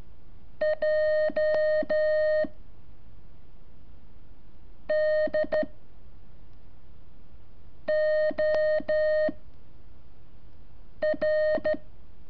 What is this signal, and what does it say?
Morse code, heard as a keyed tone, about 13 wpm, 620 Hz. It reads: J D O R
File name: curso / ejercicio 25